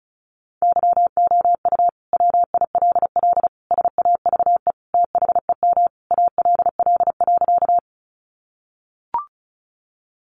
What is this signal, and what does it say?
Morse code, heard as a keyed tone, about 34 wpm, 700 Hz. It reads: YOU WILL SAVE THEM ALL.
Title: Sentences from Top 1100 Words 35wpm